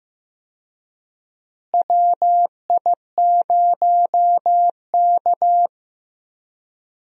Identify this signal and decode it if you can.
Morse code, heard as a keyed tone, about 15 wpm, 700 Hz. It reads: WI0K
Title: USA Call Signs - 4 Characters - Repeated 2x - 15wpm